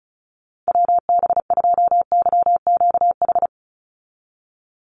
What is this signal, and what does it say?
Morse code, heard as a keyed tone, about 34 wpm, 700 Hz. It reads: WB2YQH
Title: Call Signs Repeated 2x - 35wpm